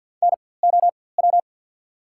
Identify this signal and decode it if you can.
Morse code, heard as a keyed tone, about 54 wpm, spaced as for 24 wpm, 690 Hz. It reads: NOW
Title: 4x Character Spacing - Top 100 Words - 50wpm